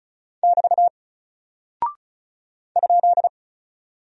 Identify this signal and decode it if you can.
Morse code, heard as a keyed tone, about 36 wpm, 700 Hz. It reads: <BT> ?
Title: Single Character (Letter:Number:Punctuation:Prosign) 35wpm